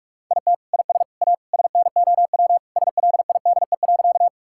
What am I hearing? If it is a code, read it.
Morse code, heard as a keyed tone, about 46 wpm, 700 Hz. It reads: IT IS A SNOW SLIDE.
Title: Sentences from Top 1600 Words 45wpm